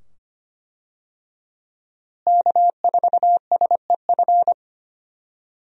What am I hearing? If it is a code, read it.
Morse code, heard as a keyed tone, about 25 wpm, 700 Hz. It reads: K4SEF
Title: USA Call Signs - 5 Characters - Repeated 2x - 25wpm